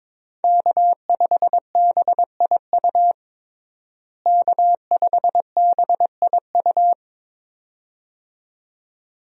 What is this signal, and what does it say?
Morse code, heard as a keyed tone, about 22 wpm, 700 Hz. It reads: K5BIU K5BIU
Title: USA Call Signs - 5 Characters - Repeated 2x - 22wpm